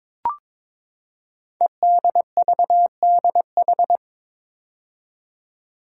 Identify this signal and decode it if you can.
Morse code, heard as a keyed tone, about 22 wpm, 700 Hz. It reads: EDVDH
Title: Code Groups - 5 Letters - 22wpm